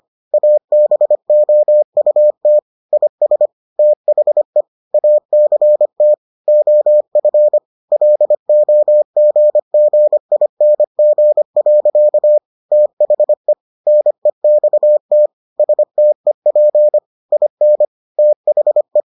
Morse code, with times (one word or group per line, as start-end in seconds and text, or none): 0.3-2.6 ABOUT
2.9-3.5 IS
3.8-4.6 THE
4.9-6.1 ACT
6.5-7.6 OF
7.9-12.4 LOGGING.
12.7-13.5 THE
13.9-15.3 NEXT
15.6-17.0 STEP
17.3-17.9 IN
18.2-19.0 THE